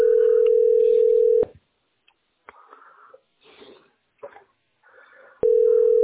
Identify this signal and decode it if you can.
Ringback tone (North America), 440 + 480 Hz.